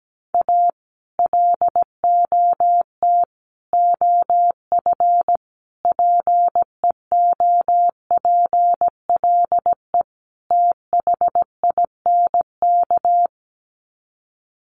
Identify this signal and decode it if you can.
Morse code, heard as a keyed tone, about 17 wpm, 700 Hz. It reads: A LOT OF PEOPLE THINK
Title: Sets of 5 Words Encore 17wpm